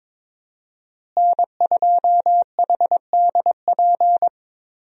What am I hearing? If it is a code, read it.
Morse code, heard as a keyed tone, about 22 wpm, 700 Hz. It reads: N2HDP
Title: USA Call Signs - 5 Characters - Repeated 2x - 22wpm